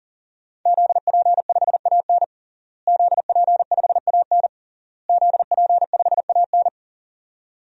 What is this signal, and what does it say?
Morse code, heard as a keyed tone, about 40 wpm, 700 Hz. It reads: ZP5AN ZP5AN ZP5AN
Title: DXCC-100 Call Signs - 5 Characters - Repeated 3x - 40wpm